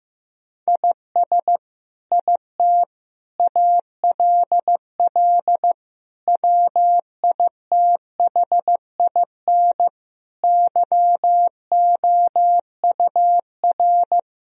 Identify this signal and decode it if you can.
Morse code, heard as a keyed tone, about 16 wpm, 700 Hz. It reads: IS IT ALL WITHIN YOUR
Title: Sentences from Top 1000 Words - Review Part 2 - 15wpm